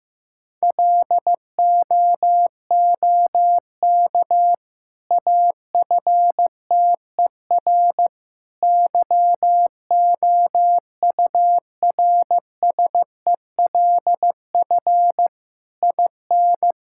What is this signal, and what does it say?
Morse code, heard as a keyed tone, about 15 wpm, 700 Hz. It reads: LOOK AFTER YOURSELF IN
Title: Sentences from Top 1000 Words - Review Part 6 - 15wpm